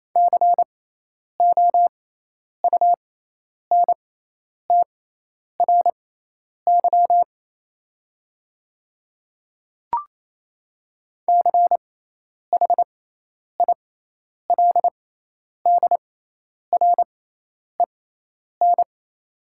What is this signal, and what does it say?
Morse code, heard as a keyed tone, about 28 wpm, spaced as for 10 wpm, 690 Hz. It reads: COUNTRY CHILDREN
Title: 6x Character Spacing - Top 500 Words - 28wpm